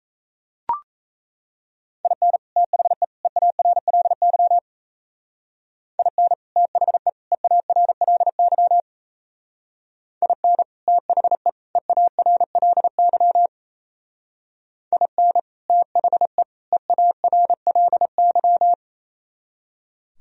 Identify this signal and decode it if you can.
Morse code, heard as a keyed tone, about 35 wpm, 700 Hz. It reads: IN THE EARLY IN THE EARLY IN THE EARLY IN THE EARLY